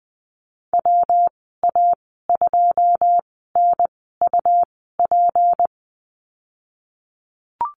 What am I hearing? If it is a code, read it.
Morse code, heard as a keyed tone, about 20 wpm, spaced as for 14 wpm, 700 Hz. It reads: WA2NUP
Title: 173 Lesson 28g - Number 9 - Call Signs 20wpm